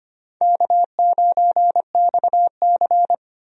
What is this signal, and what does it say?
Morse code, heard as a keyed tone, about 25 wpm, 700 Hz. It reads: K9XC
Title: USA Call Signs - 4 Characters - 25wpm